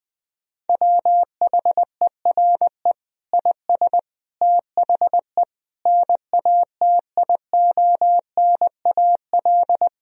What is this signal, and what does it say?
Morse code, heard as a keyed tone, about 20 wpm, 700 Hz. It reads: WHERE IS THE NATIONAL
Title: Sentences from Top 1000 Words - Review Part 4 - 20wpm